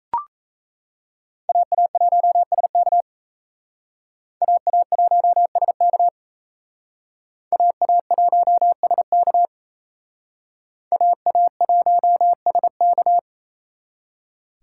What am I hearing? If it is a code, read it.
Morse code, heard as a keyed tone, about 36 wpm, 700 Hz. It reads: AA1SK AA1SK AA1SK AA1SK